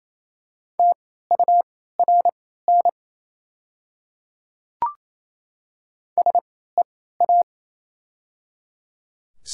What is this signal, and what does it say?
Morse code, heard as a keyed tone, about 28 wpm, spaced as for 16 wpm, 700 Hz. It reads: TURN SEA
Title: 3x Character Spacing - Top 300 Words - 28wpm